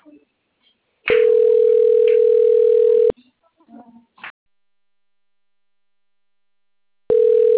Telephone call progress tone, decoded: ringback tone

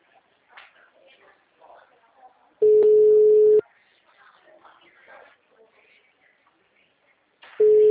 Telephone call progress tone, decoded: ringback tone

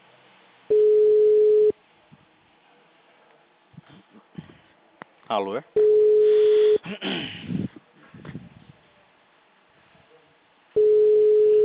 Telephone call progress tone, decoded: ringback tone

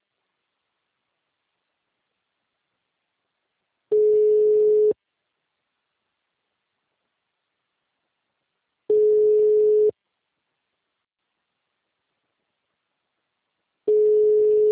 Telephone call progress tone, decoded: ringback tone

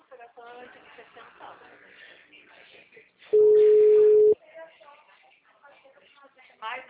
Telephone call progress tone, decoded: ringback tone

